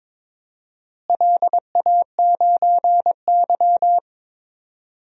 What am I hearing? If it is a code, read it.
Morse code, heard as a keyed tone, about 22 wpm, 700 Hz. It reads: LA9Y
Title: DXCC-100 Call Signs - 4 Characters - Repeated 3x - 22wpm